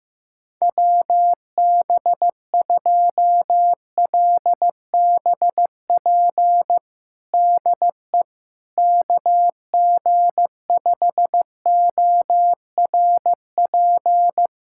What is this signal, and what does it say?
Morse code, heard as a keyed tone, about 15 wpm, 700 Hz. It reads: WB2LBP DE KG5ORP
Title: USA Callsign DE Callsign - 6 Characters - 15wpm